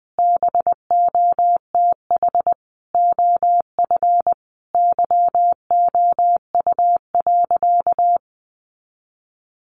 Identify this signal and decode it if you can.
Morse code, heard as a keyed tone, about 20 wpm, 700 Hz. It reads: BOTH OF YOU.
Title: Sentences from Top 1200 Words 20wpm